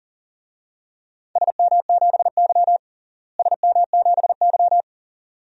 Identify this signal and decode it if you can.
Morse code, heard as a keyed tone, about 37 wpm, 700 Hz. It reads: SM7Y SM7Y
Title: DXCC-100 Call Signs - 4 Characters - Repeated 2x - 40wpm